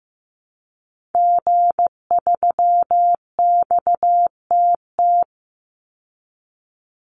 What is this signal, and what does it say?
Morse code, heard as a keyed tone, about 15 wpm, 700 Hz. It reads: G3XTT
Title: Call Signs Repeated 2x - 15wpm